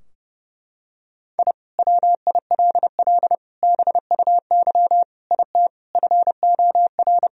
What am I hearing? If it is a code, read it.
Morse code, heard as a keyed tone, about 30 wpm, 700 Hz. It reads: I WILL BUY IT FOR
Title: Sentences from Top 1000 Words - Review Part 4 - 30wpm